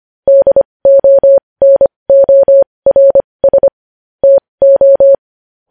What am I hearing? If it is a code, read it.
Morse code, heard as a keyed tone, about 25 wpm, spaced as for 20 wpm, 550 Hz. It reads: DONORS TO
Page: News Headlines in Morse Code at 20 WPM